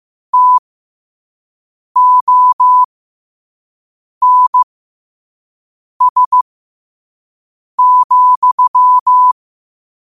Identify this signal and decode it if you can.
Morse code, heard as a keyed tone, about 15 wpm, spaced as for 5 wpm, 1000 Hz. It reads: TONS,
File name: qotdfiles / 5wpm